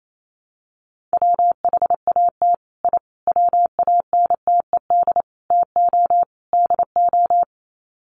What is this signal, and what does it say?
Morse code, heard as a keyed tone, about 28 wpm, 700 Hz. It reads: WHAT I WANTED TO DO
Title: Sets of 5 Words Encore2 28wpm